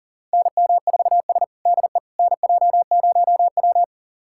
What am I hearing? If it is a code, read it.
Morse code, heard as a keyed tone, about 40 wpm, 700 Hz. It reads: NM4S DE NJ0W
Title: USA Callsign DE Callsign - 4 Characters - 40wpm